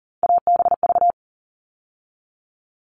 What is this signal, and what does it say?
Morse code, heard as a keyed tone, about 40 wpm, 700 Hz. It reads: ABV
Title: All Cut Numbers - 3-Digit 40wpm